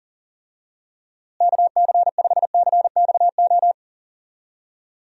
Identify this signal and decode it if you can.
Morse code, heard as a keyed tone, about 37 wpm, 700 Hz. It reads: KC5CXO
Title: USA Call Signs - 6 Characters - Repeated 2x - 40wpm